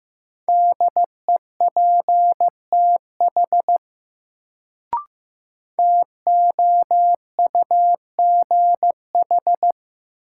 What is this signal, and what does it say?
Morse code, heard as a keyed tone, about 15 wpm, 700 Hz. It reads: DEPTH TOUGH
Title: Sentences from Top 1200 Words - New Word Drill - 15wpm